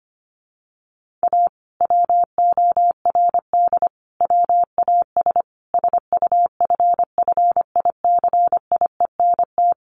Morse code, heard as a keyed tone, about 25 wpm, 700 Hz. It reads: A WORD WAS SUFFICIENT
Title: Sentences from Top 1400 Words 25wpm